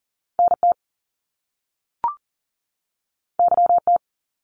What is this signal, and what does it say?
Morse code, heard as a keyed tone, about 40 wpm, 700 Hz. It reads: NT YT
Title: Canadian Provinces and Territories 40wpm